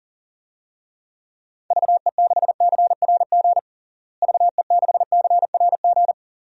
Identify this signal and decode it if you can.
Morse code, heard as a keyed tone, about 40 wpm, 700 Hz. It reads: VE6CRG VE6CRG
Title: DXCC-100 Call Signs - 6 Characters - Repeated 3x - 40wpm